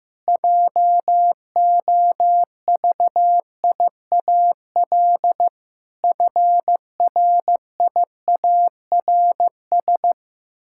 Morse code, heard as a keyed tone, about 15 wpm, 700 Hz. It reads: JOVIAL FRIARS
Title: Pangram Challenge v2 - 15wpm